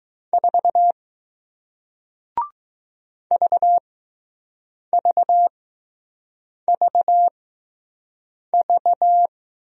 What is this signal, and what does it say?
Morse code, heard as a keyed tone, about 20 wpm, spaced as for 6 wpm, 700 Hz. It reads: 4 VVVV